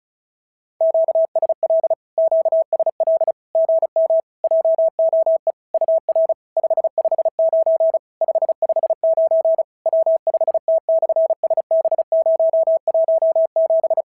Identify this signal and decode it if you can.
Morse code, heard as a keyed tone, about 36 wpm, 690 Hz. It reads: QSL QSL GM JOE UR 559 559 W5T/SB017
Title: SOTA QSO Practice 35wpm